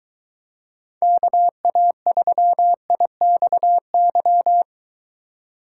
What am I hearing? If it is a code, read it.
Morse code, heard as a keyed tone, about 22 wpm, 700 Hz. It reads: KA3IXY